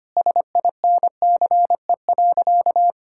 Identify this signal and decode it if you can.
Morse code, heard as a keyed tone, about 25 wpm, 700 Hz. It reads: SINCE.
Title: Sentences from Top 1600 Words 25wpm